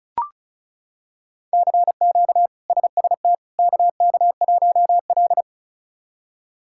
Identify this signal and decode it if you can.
Morse code, heard as a keyed tone, about 36 wpm, 700 Hz. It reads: CQ SST KK1L
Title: K1USN SST QSO Practice - 35wpm